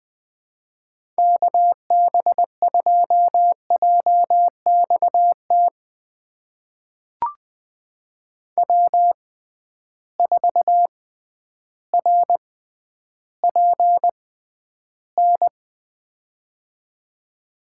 Morse code, heard as a keyed tone, about 20 wpm, spaced as for 11 wpm, 700 Hz. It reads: KB2JXT W4RPN
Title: USA Call Signs - 6x Character Spacing - 20wpm